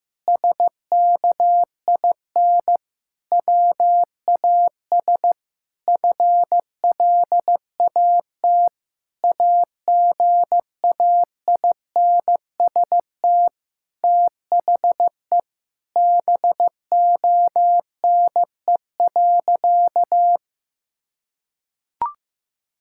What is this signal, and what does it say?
Morse code, heard as a keyed tone, about 15 wpm, 700 Hz. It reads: SKIN WAS FLAT AGAINST THE BONE.
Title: Sentences from Top 1600 Words 15wpm